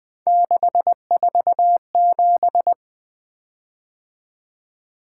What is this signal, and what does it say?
Morse code, heard as a keyed tone, about 20 wpm, 700 Hz. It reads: 647
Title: Three Letter-Number Combinations - 20wpm